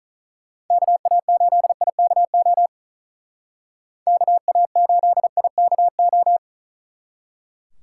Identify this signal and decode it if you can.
Morse code, heard as a keyed tone, about 44 wpm, 700 Hz. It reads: KA8IKO KA8IKO